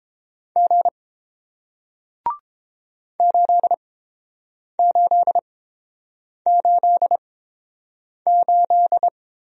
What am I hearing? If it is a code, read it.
Morse code, heard as a keyed tone, about 32 wpm, spaced as for 7 wpm, 700 Hz. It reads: G 8888